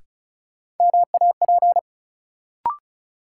Morse code, heard as a keyed tone, about 36 wpm, 700 Hz. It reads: MAP
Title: Sentences from Top 400 Words - New Word Drill - 35wpm